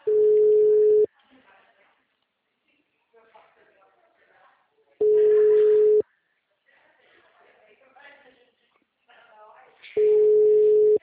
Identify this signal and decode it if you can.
Ringback tone (Europe, 425 Hz), with speech.
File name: out- -2012- - - .194